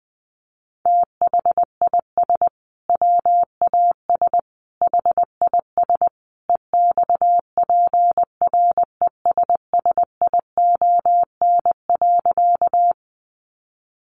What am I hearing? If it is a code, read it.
Morse code, heard as a keyed tone, about 20 wpm, 700 Hz. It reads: THIS WAS HIS EXPRESSION.
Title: Sentences from Top 1200 Words 20wpm